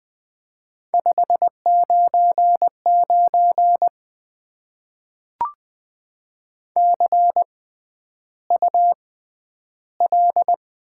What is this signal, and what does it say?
Morse code, heard as a keyed tone, about 20 wpm, spaced as for 10 wpm, 700 Hz. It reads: 599 CUL
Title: Top 90 QSO Elements - 6x Character Spacing - 20wpm